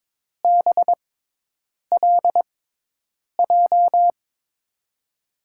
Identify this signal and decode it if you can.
Morse code, heard as a keyed tone, about 23 wpm, spaced as for 8 wpm, 700 Hz. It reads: BLJ
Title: USA Call Signs - 6x Character Spacing - 22wpm